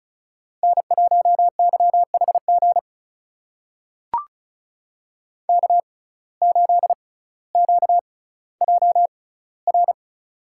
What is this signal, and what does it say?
Morse code, heard as a keyed tone, about 36 wpm, spaced as for 18 wpm, 700 Hz. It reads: N1YHG K8QJR
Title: USA Call Signs - 5 Characters - 6x Character Spacing - 35wpm